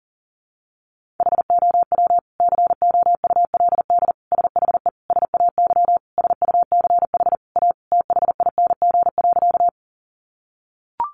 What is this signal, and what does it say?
Morse code, heard as a keyed tone, about 40 wpm, 700 Hz. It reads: HOW COULD SHE SAY SUCH A THING.
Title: Sentences from Top 200 Words 40wpm